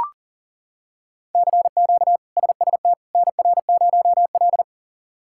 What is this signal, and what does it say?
Morse code, heard as a keyed tone, about 40 wpm, 700 Hz. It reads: CQ SST NR0L
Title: K1USN SST QSO Practice - 40wpm